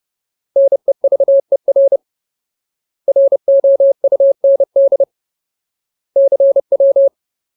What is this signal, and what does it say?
Morse code, heard as a keyed tone, about 30 wpm, 550 Hz. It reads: NEVER ROUND CW
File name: Ultimate Word List / 30WPM Ultimate